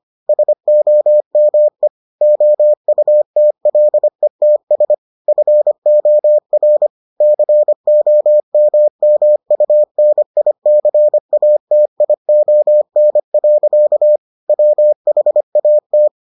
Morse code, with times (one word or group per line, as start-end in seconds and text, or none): 0.3-1.9 SOME
2.2-4.9 OUTLETS
5.3-6.9 FOR
7.2-14.2 COMMUNICATION.
14.5-16.1 WHAT